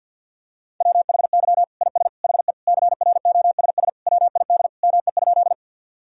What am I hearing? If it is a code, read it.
Morse code, heard as a keyed tone, about 52 wpm, 700 Hz. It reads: WHY IS HE CROSS WID ME?
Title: Sentences from Top 300 Words and Abbreviations 50wpm